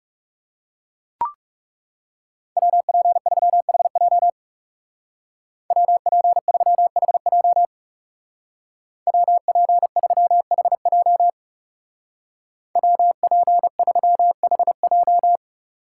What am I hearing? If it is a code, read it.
Morse code, heard as a keyed tone, about 37 wpm, 700 Hz. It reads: WP3HJ WP3HJ WP3HJ WP3HJ
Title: Speed Racing - USA 2x2 Call Signs 30wpm